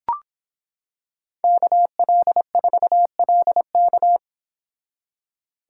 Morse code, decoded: KL4LK

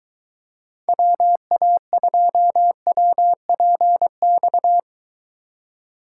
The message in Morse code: WA2WPX

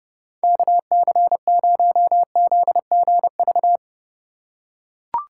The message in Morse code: KC0ZGV

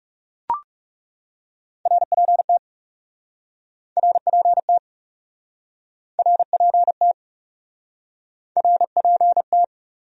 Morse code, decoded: RPT RPT RPT RPT

